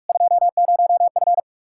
Morse code, heard as a keyed tone, about 61 wpm, spaced as for 53 wpm, 700 Hz. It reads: 20F
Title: Speed-Racing Common QSO Phrases - 45wpm